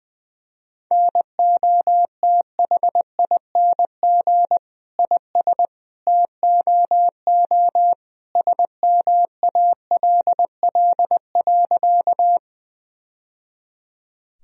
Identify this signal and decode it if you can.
Morse code, heard as a keyed tone, about 20 wpm, 700 Hz. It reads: NOTHING IS TOO SMALL.
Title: Sentences from Top 1000 Words - Review Part 1 - 20wpm